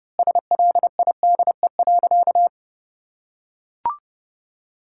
Morse code, heard as a keyed tone, about 30 wpm, 700 Hz. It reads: SLIDE.
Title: Sentences from Top 1600 Words 30wpm